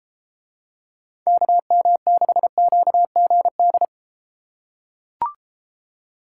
Morse code, heard as a keyed tone, about 33 wpm, 700 Hz. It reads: KM6QGD